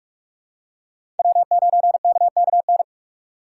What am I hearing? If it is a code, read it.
Morse code, heard as a keyed tone, about 43 wpm, 700 Hz. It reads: W9KKN
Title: USA Call Signs - 5 Characters - 6x Character Spacing - 45wpm